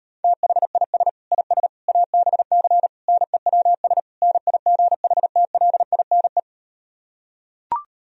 Morse code, heard as a keyed tone, about 38 wpm, 700 Hz. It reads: THIS IS ABC NEWS NIGHTLINE